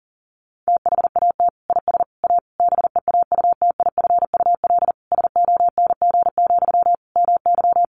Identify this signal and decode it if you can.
Morse code, heard as a keyed tone, about 37 wpm, 700 Hz. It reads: THAT IS A BEAUTIFUL SONG, MY
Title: Sentences from Top 1500 Words 40wpm